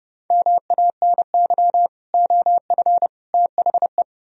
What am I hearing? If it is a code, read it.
Morse code, heard as a keyed tone, about 38 wpm, 700 Hz. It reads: MANY OF THE